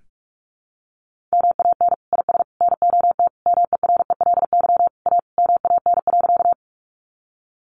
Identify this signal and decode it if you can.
Morse code, heard as a keyed tone, about 44 wpm, 700 Hz. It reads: MAN IS NOT MERELY A MAN.
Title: Sentences from Top 1400 Words 45wpm